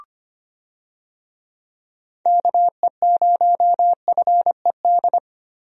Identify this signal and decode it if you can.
Morse code, heard as a keyed tone, about 25 wpm, 700 Hz. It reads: KE0FED
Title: USA Call Signs - 6 Characters - Repeated 3x - 25wpm